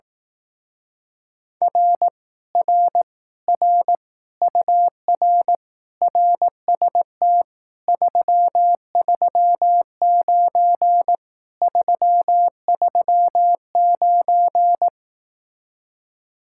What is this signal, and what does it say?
Morse code, heard as a keyed tone, about 18 wpm, 700 Hz. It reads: R R R UR RST 339 339
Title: Speed-Racing Common QSO Phrases Encore - 15wpm